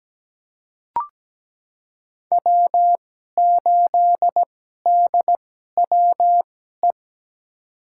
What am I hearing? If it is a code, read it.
Morse code, heard as a keyed tone, about 17 wpm, spaced as for 12 wpm, 700 Hz. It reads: W8DWE
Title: USA Call Signs - 5 Characters - 2x Character Spacing - 17wpm